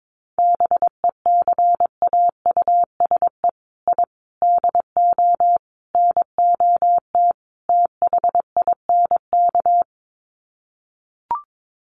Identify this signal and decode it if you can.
Morse code, heard as a keyed tone, about 22 wpm, 700 Hz. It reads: BECAUSE I DO NOT THINK
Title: Sets of 5 Words Encore 22wpm